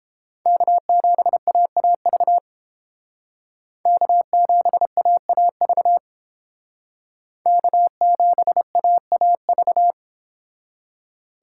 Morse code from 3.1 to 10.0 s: K7AAV K7AAV

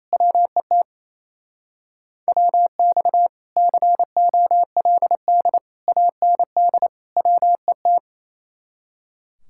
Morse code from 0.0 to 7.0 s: WET WX COLD AND

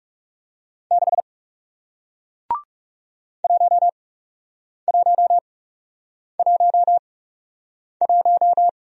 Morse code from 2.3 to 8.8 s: 1111

